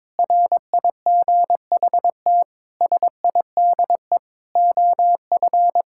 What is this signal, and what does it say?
Morse code, heard as a keyed tone, about 22 wpm, 700 Hz. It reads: RIGHT SIDE OF
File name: practice / 22wpm / Sentences with 8-Word Phrases - 22wpm